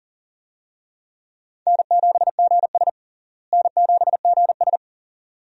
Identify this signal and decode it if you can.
Morse code, heard as a keyed tone, about 40 wpm, 700 Hz. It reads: N7GS N7GS
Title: USA Call Signs - 4 Characters - Repeated 2x - 40wpm